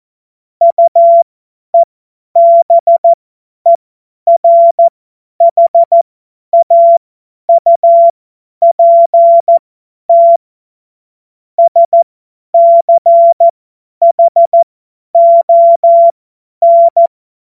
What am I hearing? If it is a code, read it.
Morse code, heard as a keyed tone, about 14 wpm, spaced as for 10 wpm, 680 Hz. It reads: UEBERHAUPT SCHON
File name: Morsekurs / Witze / Witz 01. März 2023 50 F70